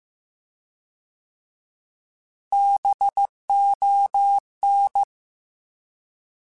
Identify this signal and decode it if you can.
Morse code, heard as a keyed tone, about 15 wpm, 800 Hz. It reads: BON